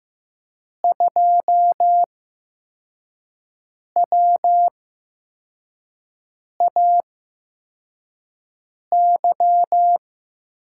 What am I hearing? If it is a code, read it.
Morse code, heard as a keyed tone, about 15 wpm, spaced as for 4 wpm, 700 Hz. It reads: 2WAY
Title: USA Call Signs - 5 Characters - 8x Character Spacing - 15wpm